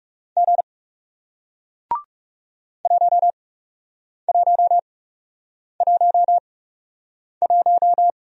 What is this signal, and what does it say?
Morse code, heard as a keyed tone, about 41 wpm, spaced as for 8 wpm, 700 Hz. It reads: G 1111